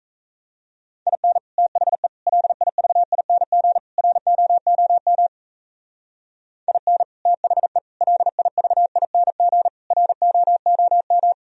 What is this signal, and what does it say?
Morse code, heard as a keyed tone, about 34 wpm, 700 Hz. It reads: IN THE LIVING ROOM IN THE LIVING ROOM
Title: Speed-Racing - Sets of 4 Words 28wpm